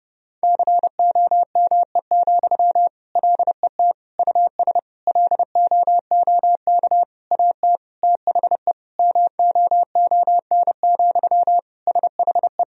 Morse code, heard as a keyed tone, about 30 wpm, 700 Hz. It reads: COME, LET US LOOK AT THE MOON, SHE
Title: Sentences from Top 1000 Words - Review Part 3 - 30wpm